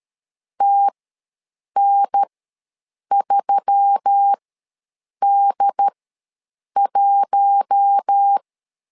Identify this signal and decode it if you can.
Morse code, heard as a keyed tone, about 13 wpm, spaced as for 7 wpm, 800 Hz. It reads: TN3D1
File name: LEZIONI / 4 Q1S3 ws 12wpm 2038